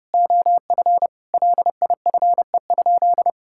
Morse code, heard as a keyed tone, about 30 wpm, 700 Hz. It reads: OF LIFE?
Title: Sentences from Top 1000 Words - Review Part 5 - 30wpm